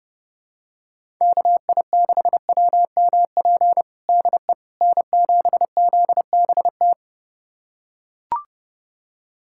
Morse code, heard as a keyed tone, about 30 wpm, 700 Hz. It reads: KI6WMP DE N7ZBT
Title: USA Callsign DE Callsign 30wpm